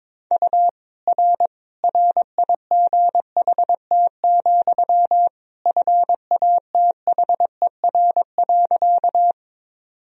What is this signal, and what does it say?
Morse code, heard as a keyed tone, about 22 wpm, 700 Hz. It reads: U R RIGHT, FATHER.
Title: Sentences from Top 300 Words and Abbreviations 22wpm